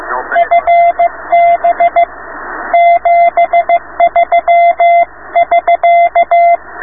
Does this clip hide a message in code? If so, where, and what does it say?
Yes: Morse code from 0.4 to 6.6 s, heard as a keyed tone, about 15 wpm, 690 Hz, with radio noise. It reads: FB 73<SK>